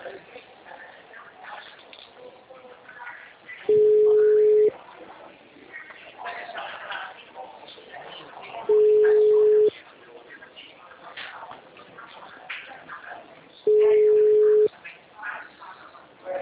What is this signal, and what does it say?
Ringback tone (Europe, 425 Hz), with speech.
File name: out- -2015- - - .343